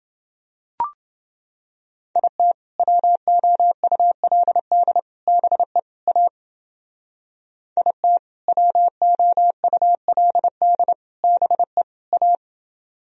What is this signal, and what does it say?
Morse code, heard as a keyed tone, about 30 wpm, 700 Hz. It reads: IT WOULD BE A IT WOULD BE A